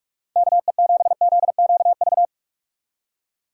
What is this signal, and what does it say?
Morse code, heard as a keyed tone, about 44 wpm, 700 Hz. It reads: KE7ZQV
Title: USA Call Signs - 3x Character Spacing - 45wpm